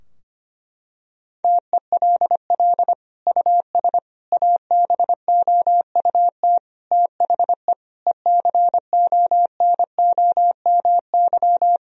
Morse code, heard as a keyed tone, about 25 wpm, 700 Hz. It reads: TELL US ABOUT THE ECONOMY
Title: Sentences from Top 1000 Words - Review Part 3 - 25wpm